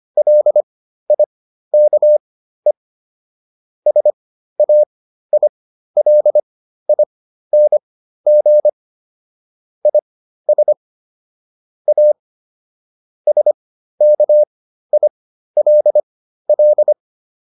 Morse code, heard as a keyed tone, about 25 wpm, spaced as for 13 wpm, 600 Hz. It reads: LIKE SAILING IS A SKILL